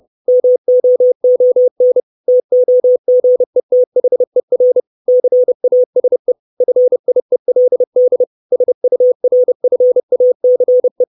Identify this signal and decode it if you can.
Morse code, heard as a keyed tone, about 30 wpm, 500 Hz. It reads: MOON TOGETHER CASE FIELD SURFACE
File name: 632 Words 30WPM